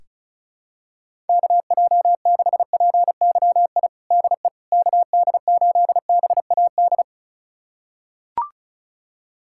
Morse code, heard as a keyed tone, about 34 wpm, 700 Hz. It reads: KJ6PYI DE KD8BAD
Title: USA Callsign DE Callsign - 6 Characters - 35wpm